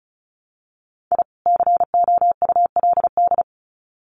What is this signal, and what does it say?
Morse code, heard as a keyed tone, about 34 wpm, 700 Hz. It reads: I COULD
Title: Sets of 2 Words 35wpm